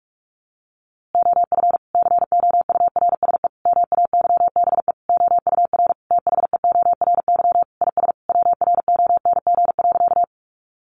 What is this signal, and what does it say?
Morse code, heard as a keyed tone, about 44 wpm, 700 Hz. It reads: OF COURSE MAYBE OUR THEORY IS WRONG.
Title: A Sound Of Thunder 45wpm